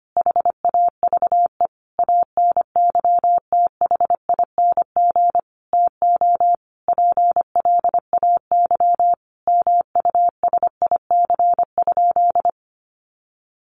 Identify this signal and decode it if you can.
Morse code, heard as a keyed tone, about 25 wpm, 700 Hz. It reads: HAVE ANYTHING TO PLAY MUSIC?
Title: Sentences from Top 1000 Words 25wpm